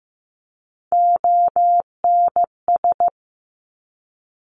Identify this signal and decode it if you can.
Morse code, heard as a keyed tone, about 15 wpm, 700 Hz. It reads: ONS
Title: Top Three Letter Combinations in Words 15wpm